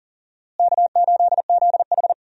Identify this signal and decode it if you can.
Morse code, heard as a keyed tone, about 37 wpm, 700 Hz. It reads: K8ZH
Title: USA Call Signs - 4 Characters - Repeated 3x - 40wpm